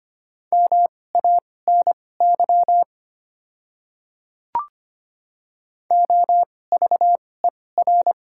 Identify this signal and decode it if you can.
Morse code, heard as a keyed tone, about 25 wpm, spaced as for 18 wpm, 690 Hz. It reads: MANY OVER
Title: 2x Character Spacing - Top 100 Words - 25wpm